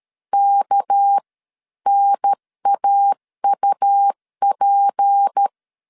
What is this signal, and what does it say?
Morse code, heard as a keyed tone, about 13 wpm, spaced as for 11 wpm, 800 Hz. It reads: KNAUP
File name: CW / DET / 1 6 1500C S C 12wpm 3533